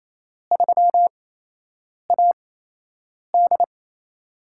Morse code, heard as a keyed tone, about 27 wpm, spaced as for 8 wpm, 700 Hz. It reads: 3AD
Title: USA Call Signs - 4 Characters - 8x Character Spacing - 28wpm